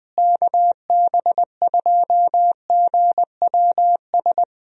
Morse code, heard as a keyed tone, about 20 wpm, 700 Hz. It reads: KB2GWS
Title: USA Call Signs - 6 Characters - Repeated 2x - 20wpm